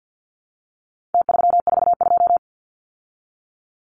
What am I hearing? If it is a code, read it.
Morse code, heard as a keyed tone, about 50 wpm, 700 Hz. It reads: T342
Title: Numbers - Contest Style - 4-Digit 50wpm